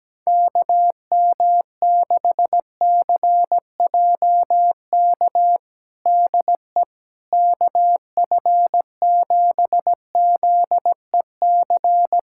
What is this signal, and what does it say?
Morse code, heard as a keyed tone, about 17 wpm, 700 Hz. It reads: KM6CJK DE KF7ZEC
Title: USA Callsign DE Callsign - 6 Characters - 17wpm